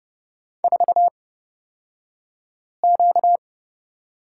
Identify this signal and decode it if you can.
Morse code, heard as a keyed tone, about 30 wpm, spaced as for 5 wpm, 700 Hz. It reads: 4Q
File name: Lesson 38d - Letter q - Two Characters 30wpm